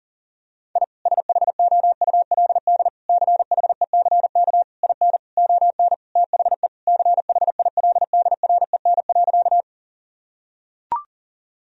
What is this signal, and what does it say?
Morse code, heard as a keyed tone, about 37 wpm, 700 Hz. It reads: I SHOULD CHECK IN ON THE CHILDREN.
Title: Sentences from Top 1000 Words - Review Part 3 - 40wpm